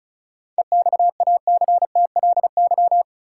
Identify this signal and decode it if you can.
Morse code, heard as a keyed tone, about 36 wpm, 700 Hz. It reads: EXACTLY
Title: Sentences from Top 900 Words - New Word Drill - 35wpm